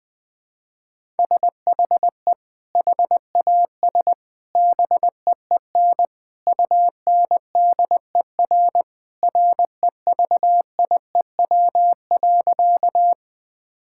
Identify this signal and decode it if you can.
Morse code, heard as a keyed tone, about 21 wpm, 700 Hz. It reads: SHE HAS BEEN UNDER REVIEW.
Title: Sentences from Top 1000 Words - Review Part 6 - 20wpm